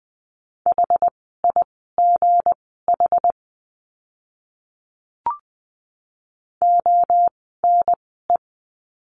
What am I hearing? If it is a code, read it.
Morse code, heard as a keyed tone, about 20 wpm, spaced as for 14 wpm, 700 Hz. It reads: HIGH ONE
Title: 237 Lesson 38f - Letter q - Words 20wpm